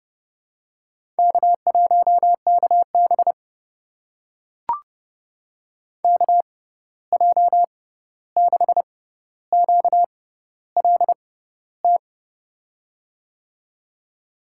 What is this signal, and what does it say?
Morse code, heard as a keyed tone, about 31 wpm, spaced as for 14 wpm, 700 Hz. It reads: K1KB KJ6QLT
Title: USA Call Signs - 6x Character Spacing - 30wpm